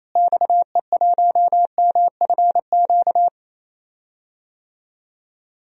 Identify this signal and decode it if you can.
Morse code, heard as a keyed tone, about 28 wpm, 700 Hz. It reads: XE1MFQ
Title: DXCC-100 Call Signs - 6 Characters - Repeated 2x - 28wpm